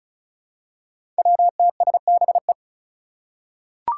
Morse code, heard as a keyed tone, about 36 wpm, 700 Hz. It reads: WTSBE
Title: Five Challenging Letter-Number Combinations 35wpm